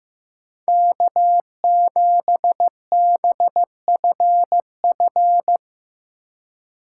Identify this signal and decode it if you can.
Morse code, heard as a keyed tone, about 15 wpm, 700 Hz. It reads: K7BFF
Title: USA Call Signs - 5 Characters - 4x Character Spacing - 15wpm